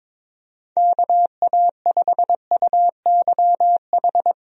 Morse code, heard as a keyed tone, about 21 wpm, 700 Hz. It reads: KA5UYH
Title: USA Call Signs - 6 Characters - Repeated 3x - 22wpm